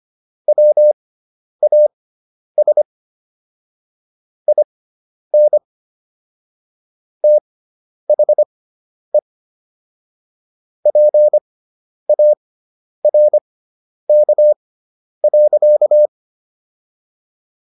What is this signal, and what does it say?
Morse code, heard as a keyed tone, about 25 wpm, spaced as for 10 wpm, 600 Hz. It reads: WAS IN THE PARK.